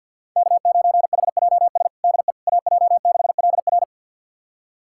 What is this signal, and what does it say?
Morse code, heard as a keyed tone, about 53 wpm, 700 Hz. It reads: K9HJS DE AJ6LR
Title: USA Callsign DE Callsign - 5 Characters - 50wpm